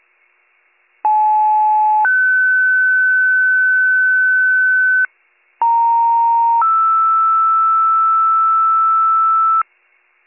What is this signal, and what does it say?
Two-tone page, heard as a one-second tone, then a three-second tone, about 860 then 1500 Hz, about 930 then 1320 Hz. It is heard from 1.0 to 9.6 s.